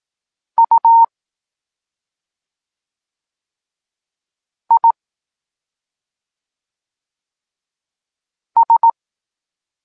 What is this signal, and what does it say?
Morse code, heard as a keyed tone, about 18 wpm, spaced as for 2 wpm, 920 Hz. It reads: UIS
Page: Tekst 8 – doskonalenie odbioru poznanych liter